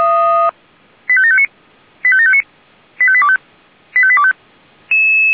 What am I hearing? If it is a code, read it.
Five-tone selective call (ZVEI), heard as quick runs of five tones, tones of about 70 ms. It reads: 85759 85759 85825 85825